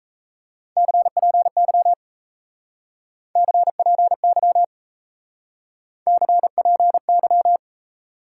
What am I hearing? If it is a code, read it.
Morse code, heard as a keyed tone, about 37 wpm, 700 Hz. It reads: CPY CPY CPY